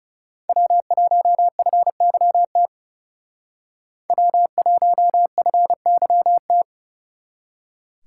Morse code, heard as a keyed tone, about 38 wpm, 700 Hz. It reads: W1FYT W1FYT